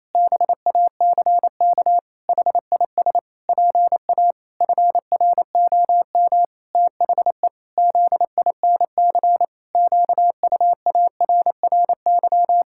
Morse code, heard as a keyed tone, about 28 wpm, 700 Hz. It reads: BACK HIS PA FROM THE ZINC QUARRY
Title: Pangram Challenge v2 - 28wpm